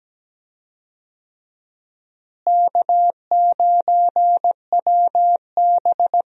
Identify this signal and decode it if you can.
Morse code, heard as a keyed tone, about 17 wpm, 700 Hz. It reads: K9WB
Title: USA Call Signs - 4 Characters - Repeated 2x - 17wpm